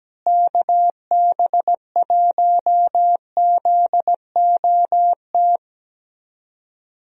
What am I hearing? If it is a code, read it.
Morse code, heard as a keyed tone, about 17 wpm, 700 Hz. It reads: KB1ZOT
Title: USA Call Signs - 6 Characters - Repeated 3x - 17wpm